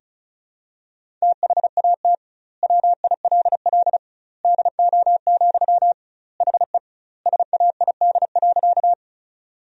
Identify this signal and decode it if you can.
Morse code, heard as a keyed tone, about 34 wpm, 700 Hz. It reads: THAT WILL DO, HE SAID.
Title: 2x Word-Spacing - Sentences from Top 100 Words - 35wpm